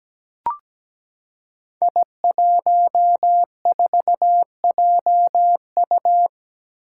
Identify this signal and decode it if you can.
Morse code, heard as a keyed tone, about 17 wpm, 700 Hz. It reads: I14JU
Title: Five Challenging Letter-Number Combinations 17wpm